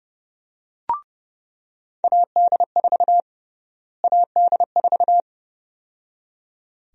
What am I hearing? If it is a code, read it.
Morse code, heard as a keyed tone, about 30 wpm, 700 Hz. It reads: AD4 AD4
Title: USA Call Sign Prefixes - Repeated 2x - 30wpm